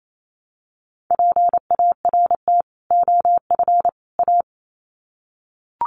Morse code, heard as a keyed tone, about 28 wpm, 700 Hz. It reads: PART OF A
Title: Sets of 3 Words 28wpm